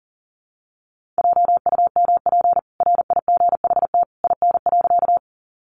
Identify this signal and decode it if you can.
Morse code, heard as a keyed tone, about 40 wpm, 700 Hz. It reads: JUMP RIGHT IN.
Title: Sentences from Top 1300 Words 40wpm